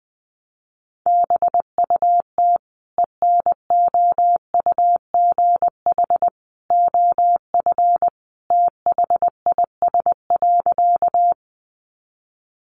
Morse code, heard as a keyed tone, about 20 wpm, 700 Hz. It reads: BUT ENOUGH OF THIS.
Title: Sentences from Top 400 Words 20wpm